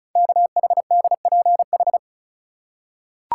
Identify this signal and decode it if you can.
Morse code, heard as a keyed tone, about 36 wpm, 700 Hz. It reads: KHDPH
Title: Code Groups - 5 Letters - 35wpm